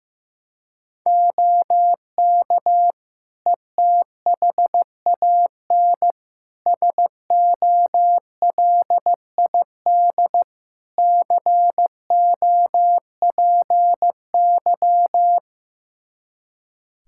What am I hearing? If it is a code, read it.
Morse code, heard as a keyed tone, about 16 wpm, 700 Hz. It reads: OK ETHAN SOLID COPY
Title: Practice QSO - Exchange 3 - More Info - 15wpm